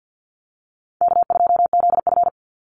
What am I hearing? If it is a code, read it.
Morse code, heard as a keyed tone, about 50 wpm, 700 Hz. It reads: K2ZF
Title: Easy Call Signs Repeated 3x - 50wpm